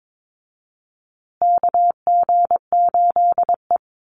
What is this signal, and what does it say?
Morse code, heard as a keyed tone, about 22 wpm, 700 Hz. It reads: KG8E
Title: USA Call Signs Repeated 3x 22wpm